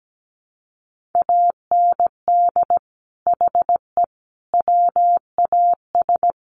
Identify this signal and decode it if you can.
Morse code, heard as a keyed tone, about 17 wpm, 700 Hz. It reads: AND HE WAS
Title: Sentences from Top 1400 Words 17wpm